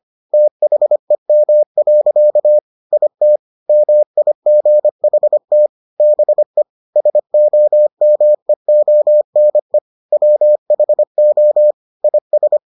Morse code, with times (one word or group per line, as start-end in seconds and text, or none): 0.3-2.6 THEM.
2.9-3.4 IT
3.7-5.7 MIGHT
6.0-6.6 BE
7.0-9.8 SOMEONE
10.1-11.7 WHO
12.0-12.6 IS